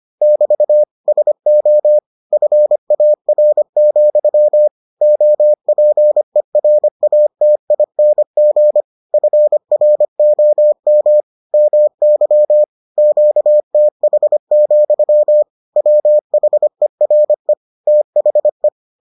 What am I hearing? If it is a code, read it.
Morse code, heard as a keyed tone, about 25 wpm, 600 Hz. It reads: <BT> SO FAR, OPERATING FROM MY QTH, WHERE THE